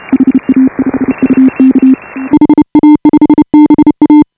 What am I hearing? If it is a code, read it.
Morse code, heard as a keyed tone, about 34 wpm, 280 Hz. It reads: HA5UK HA5BA
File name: AUDIO2018 / HA5BA CW